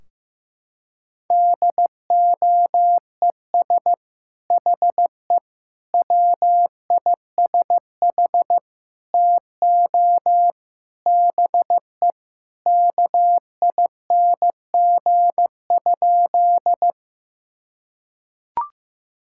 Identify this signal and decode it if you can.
Morse code, heard as a keyed tone, about 15 wpm, 700 Hz. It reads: DOES HE WISH TO BE KING?
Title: Sentences from Top 1600 Words 15wpm